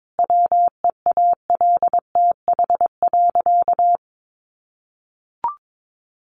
Morse code, heard as a keyed tone, about 22 wpm, 700 Hz. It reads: WEALTH.
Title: Sentences from Top 1400 Words 22wpm